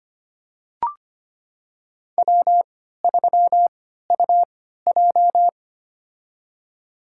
Morse code, heard as a keyed tone, about 25 wpm, spaced as for 14 wpm, 700 Hz. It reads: W3UJ
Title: USA Call Signs - 4 Characters - 3x Character Spacing - 25wpm